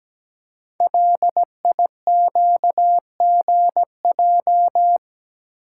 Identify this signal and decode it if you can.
Morse code, heard as a keyed tone, about 17 wpm, 700 Hz. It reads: LIQGJ
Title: Code Groups - 5 Letters - 17wpm